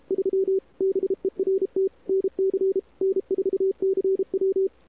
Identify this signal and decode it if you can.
Morse code, heard as a keyed tone, about 34 wpm, 410 Hz. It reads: 3 BERT NC N4CW
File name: CWSprint / Feb2007 / 2007-feb-cw-sprint-k5tr-op-n5rz -00:00